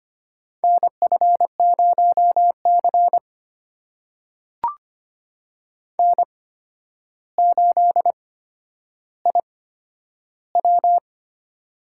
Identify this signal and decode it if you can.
Morse code, heard as a keyed tone, about 25 wpm, spaced as for 11 wpm, 700 Hz. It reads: NF0C N8IW
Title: USA Call Signs - 4 Characters - 8x Character Spacing - 25wpm